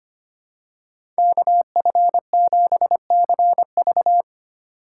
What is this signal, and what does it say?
Morse code, heard as a keyed tone, about 25 wpm, 700 Hz. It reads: KF7CV
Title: USA Call Signs - 5 Characters - 3x Character Spacing - 25wpm